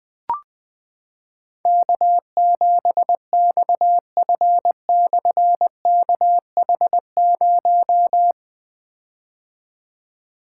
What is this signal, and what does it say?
Morse code, heard as a keyed tone, about 20 wpm, 700 Hz. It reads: K7XF/KH0
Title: Difficult Call Signs 20wpm